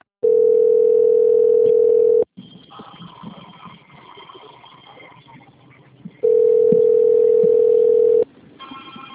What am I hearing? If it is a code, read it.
Ringback tone (North America), 440 + 480 Hz.